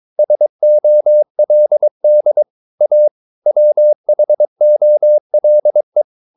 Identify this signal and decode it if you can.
Morse code, heard as a keyed tone, about 22 wpm, 600 Hz. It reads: SOLD A WHOLE
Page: Romanen Never af Ken Follet på engelsk.